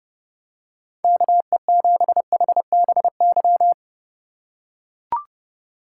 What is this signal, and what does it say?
Morse code, heard as a keyed tone, about 30 wpm, 700 Hz. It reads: KE7HBY
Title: USA Call Signs - 6 Characters - 3x Character Spacing - 30wpm